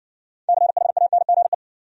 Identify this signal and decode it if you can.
Morse code, heard as a keyed tone, about 60 wpm, 700 Hz. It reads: CHANGE